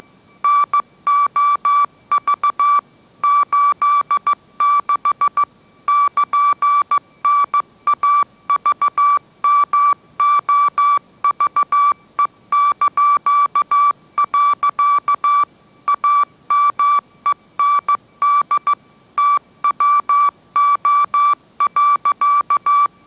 Morse code, heard as a keyed tone, about 15 wpm, 1160 Hz. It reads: NOV 86 <KN>NAVMOVE). AMEND TWO.